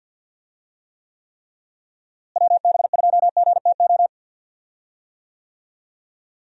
Morse code, heard as a keyed tone, about 48 wpm, 700 Hz. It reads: WB2GTO
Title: USA Call Signs - 6 Characters - Repeated 3x - 50wpm